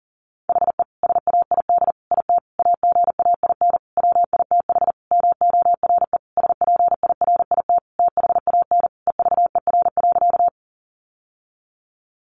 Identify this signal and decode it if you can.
Morse code, heard as a keyed tone, about 40 wpm, 700 Hz. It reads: HE SAID IT AGAIN WITH MORE SPIRIT THAN EVER.
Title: Sentences from Top 1400 Words 40wpm